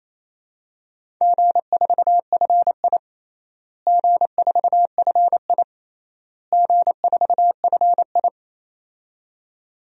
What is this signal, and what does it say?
Morse code, heard as a keyed tone, about 28 wpm, 700 Hz. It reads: G4FI G4FI G4FI
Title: DXCC-100 Call Signs - 4 Characters - Repeated 3x - 28wpm